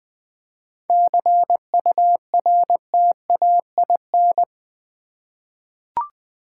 Morse code, heard as a keyed tone, about 20 wpm, 700 Hz. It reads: CURTAIN
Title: 7 Letter Words - First 3 Letters Given - 20wpm